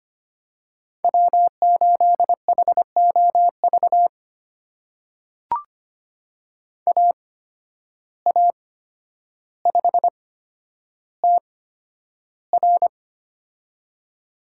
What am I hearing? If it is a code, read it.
Morse code, heard as a keyed tone, about 25 wpm, spaced as for 11 wpm, 700 Hz. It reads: W8HOV AA5TR
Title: USA Call Signs - 5 Characters - 8x Character Spacing - 25wpm